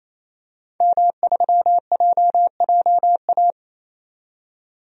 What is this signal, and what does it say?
Morse code, heard as a keyed tone, about 28 wpm, 700 Hz. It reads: M3JJA
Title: DXCC-100 Call Signs - 5 Characters - 28wpm